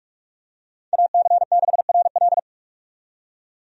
Morse code, heard as a keyed tone, about 46 wpm, 700 Hz. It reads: AC6RL
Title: USA Call Signs - 5 Characters - 45wpm